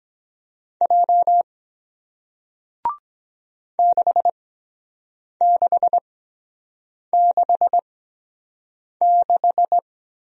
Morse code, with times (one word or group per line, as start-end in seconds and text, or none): 0.8-1.4 J
3.8-9.8 6666